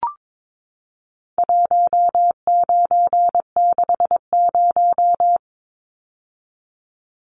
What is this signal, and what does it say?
Morse code, heard as a keyed tone, about 22 wpm, 700 Hz. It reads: 1960
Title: Numbers - 4-Digit 22wpm